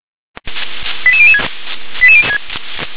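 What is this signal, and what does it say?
Five-tone selective call (ZVEI), heard as quick runs of five tones, tones of about 70 ms. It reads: 88006
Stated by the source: ZVEI five-tone selective call received via SDR.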